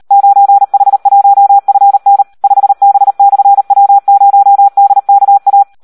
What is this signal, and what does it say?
Morse code, heard as a keyed tone, about 38 wpm, 800 Hz. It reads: 9H1FN 5B/W0DKA
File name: Audio practica CW / SD 1 38